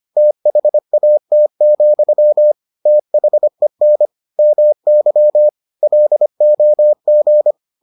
Morse code, heard as a keyed tone, about 25 wpm, 600 Hz. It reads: THAT, THEN MY LOG